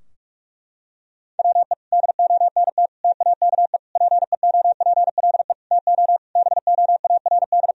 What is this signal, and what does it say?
Morse code, heard as a keyed tone, about 46 wpm, 700 Hz. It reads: WE DONT TAKE PEOPLE TO BOARD
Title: Sentences from Top 1000 Words - Review Part 4 - 45wpm